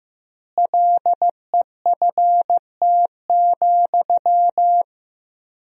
Morse code, heard as a keyed tone, about 15 wpm, 700 Hz. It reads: LEFT,
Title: 4x Word-Spacing - Sentences from Top 400 Words - 15wpm